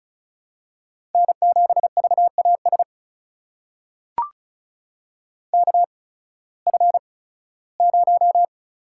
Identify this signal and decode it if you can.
Morse code, heard as a keyed tone, about 36 wpm, spaced as for 19 wpm, 700 Hz. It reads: N7VAS KF0
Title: USA Call Signs - 5 Characters - 8x Character Spacing - 35wpm